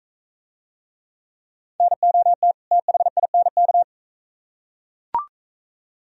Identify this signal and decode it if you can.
Morse code, heard as a keyed tone, about 41 wpm, 700 Hz. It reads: NOT THINK